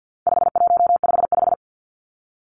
Morse code, heard as a keyed tone, about 50 wpm, 700 Hz. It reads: 5155
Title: Numbers - 4-Digit 50wpm